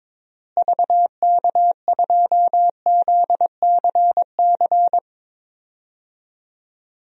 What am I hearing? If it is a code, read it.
Morse code, heard as a keyed tone, about 21 wpm, 700 Hz. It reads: VK2ZCC
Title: DXCC-100 Call Signs - 6 Characters - Repeated 2x - 22wpm